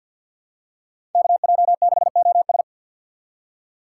Morse code, heard as a keyed tone, about 47 wpm, 700 Hz. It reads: KJ6OS
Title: USA Call Signs - 5 Characters - 3x Character Spacing - 50wpm